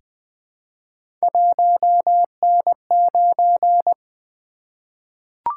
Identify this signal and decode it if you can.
Morse code, heard as a keyed tone, about 20 wpm, 700 Hz. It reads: 1N9
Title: Three Letter-Number Combinations - 20wpm